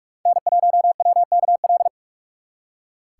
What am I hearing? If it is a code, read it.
Morse code, heard as a keyed tone, about 58 wpm, 700 Hz. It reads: N1WKL